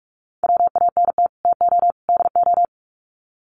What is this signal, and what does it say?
Morse code, heard as a keyed tone, about 44 wpm, 700 Hz. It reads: WANT TO DO
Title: Sets of 3 Words 45wpm